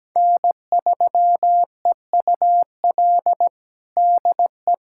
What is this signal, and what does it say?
Morse code, heard as a keyed tone, about 17 wpm, 700 Hz. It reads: N3EUL DE
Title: USA Callsign DE Callsign - 5 Characters - 17wpm